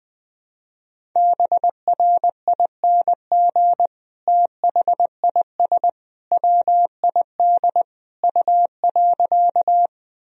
Morse code, heard as a keyed tone, about 20 wpm, 700 Hz. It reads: BRING THIS WID U.
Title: Sentences from Top 500 Words and Abbreviations 20wpm